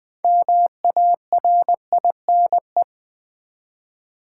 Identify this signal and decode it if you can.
Morse code, heard as a keyed tone, about 20 wpm, 700 Hz. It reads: MARINE
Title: 6 Letter Words - First 3 Letters Given - 20wpm